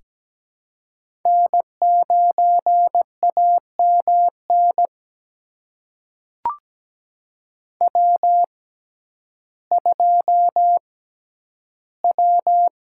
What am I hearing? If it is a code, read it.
Morse code, heard as a keyed tone, about 17 wpm, spaced as for 10 wpm, 700 Hz. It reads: N9AMN W2W
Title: USA Call Signs - 6x Character Spacing - 17wpm